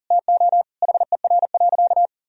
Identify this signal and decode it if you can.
Morse code, heard as a keyed tone, about 40 wpm, 700 Hz. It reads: TO HER.
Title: Sentences from Top 1000 Words - Review Part 2 - 40wpm